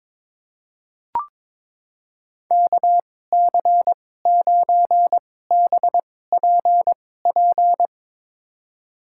Morse code, heard as a keyed tone, about 22 wpm, spaced as for 16 wpm, 700 Hz. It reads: KC9BPP
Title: USA Call Signs - 6 Characters - 2x Character Spacing - 22wpm